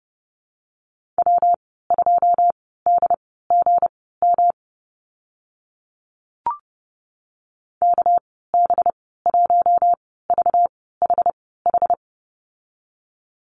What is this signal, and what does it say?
Morse code, heard as a keyed tone, about 30 wpm, spaced as for 17 wpm, 700 Hz. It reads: W2DGM KB1VHH
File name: Lesson 37g - Letter x - Call Signs 30wpm